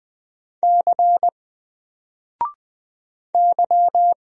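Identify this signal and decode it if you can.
Morse code, heard as a keyed tone, about 20 wpm, 700 Hz. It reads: C Y
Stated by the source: Single letter and number Morse drill at 20 wpm, from N4LCD.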